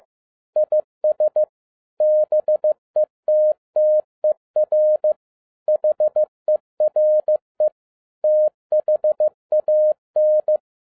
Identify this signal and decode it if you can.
Morse code, heard as a keyed tone, about 15 wpm, 610 Hz. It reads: IS BETTER HERE THAN
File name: 2011-07-mp3 / n07 A LOT HAS CHANGED 15wpm